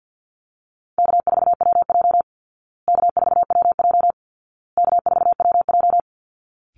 Morse code, heard as a keyed tone, about 50 wpm, 700 Hz. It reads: K4WJ K4WJ K4WJ
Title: Easy Call Signs Repeated 3x - 50wpm